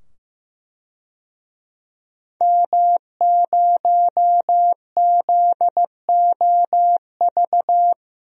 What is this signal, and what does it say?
Morse code, heard as a keyed tone, about 15 wpm, 700 Hz. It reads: M0ZOV
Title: DXCC-100 Call Signs - 5 Characters - 15wpm